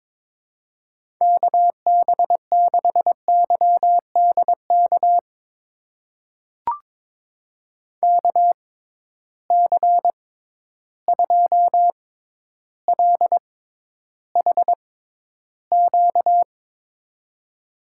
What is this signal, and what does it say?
Morse code, heard as a keyed tone, about 21 wpm, spaced as for 11 wpm, 700 Hz. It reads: KB6YDK KC2LHQ
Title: USA Call Signs - 6 Characters - 6x Character Spacing - 22wpm